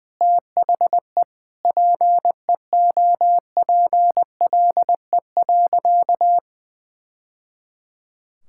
Morse code, heard as a keyed tone, about 21 wpm, 700 Hz. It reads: THE PEOPLE.
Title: Sentences from Top 1000 Words - Review Part 6 - 20wpm